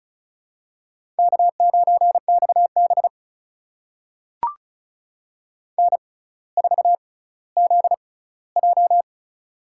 Morse code, heard as a keyed tone, about 36 wpm, spaced as for 18 wpm, 690 Hz. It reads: K9XB N4ZJ
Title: USA Call Signs - 4 Characters - 6x Character Spacing - 35wpm